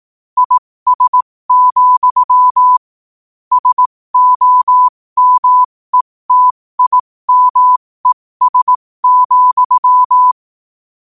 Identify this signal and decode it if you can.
Morse code, heard as a keyed tone, about 18 wpm, spaced as for 15 wpm, 1000 Hz. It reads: IS, SOMETIMES,